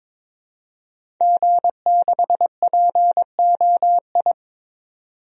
Morse code, heard as a keyed tone, about 21 wpm, 700 Hz. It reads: G6POI
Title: DXCC-100 Call Signs - 5 Characters - 22wpm